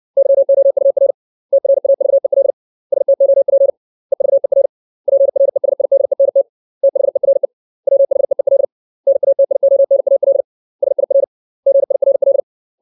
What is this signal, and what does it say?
Morse code, heard as a keyed tone, about 60 wpm, 550 Hz. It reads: YOUR TRAVEL STOP EVER PRESIDENT THERE WHEEL NATIONAL SIR GIRL
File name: 632 words / 632 Words 60WPM